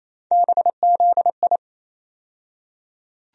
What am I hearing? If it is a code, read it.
Morse code, heard as a keyed tone, about 28 wpm, 700 Hz. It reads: BZI
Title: USA Call Sign Postfixes - 3 Letters 28wpm